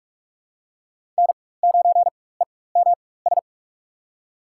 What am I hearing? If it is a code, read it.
Morse code, heard as a keyed tone, about 43 wpm, spaced as for 21 wpm, 700 Hz. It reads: N9EMS
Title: USA Call Signs - 5 Characters - 4x Character Spacing - 45wpm